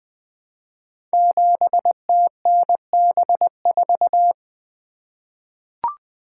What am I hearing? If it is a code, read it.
Morse code, heard as a keyed tone, about 20 wpm, 700 Hz. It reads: 7TNB4
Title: Code Groups - 5 Letter:Numbers - 20wpm